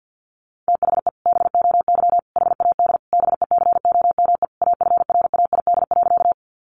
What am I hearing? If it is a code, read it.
Morse code, heard as a keyed tone, about 50 wpm, 700 Hz. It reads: THE BOY HAD BECOME AFRAID.
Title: Sentences from Top 1500 Words 50wpm